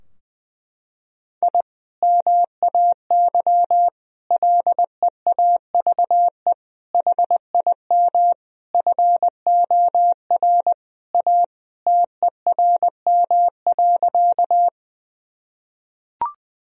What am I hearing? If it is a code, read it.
Morse code, heard as a keyed tone, about 20 wpm, 700 Hz. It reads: I MAY LEAVE HIM FOR A TERM.
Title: Sentences from Top 1000 Words - Review Part 4 - 20wpm